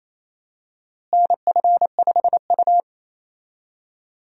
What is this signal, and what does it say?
Morse code, heard as a keyed tone, about 28 wpm, 700 Hz. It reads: NF5U
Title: USA Call Signs - 4 Characters - Repeated 2x - 28wpm